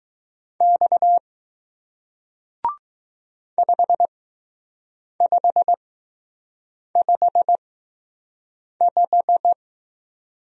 Morse code, heard as a keyed tone, about 20 wpm, spaced as for 6 wpm, 700 Hz. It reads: X 5555